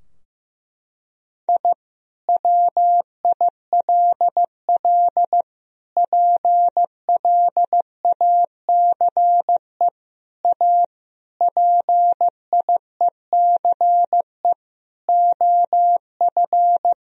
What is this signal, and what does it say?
Morse code, heard as a keyed tone, about 15 wpm, 700 Hz. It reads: I WILL PLACE A PIECE OF
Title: Sentences from Top 1000 Words - Review Part 2 - 15wpm